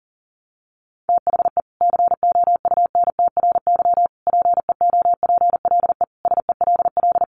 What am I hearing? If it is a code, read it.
Morse code, heard as a keyed tone, about 40 wpm, 700 Hz. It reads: THE COUNTRY PEOPLE SELL
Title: Sentences from Top 1100 Words 40wpm